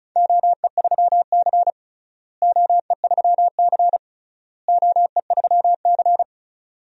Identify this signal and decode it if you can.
Morse code, heard as a keyed tone, about 36 wpm, 700 Hz. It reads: OE3C OE3C OE3C
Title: DXCC-100 Call Signs - 4 Characters - Repeated 3x - 35wpm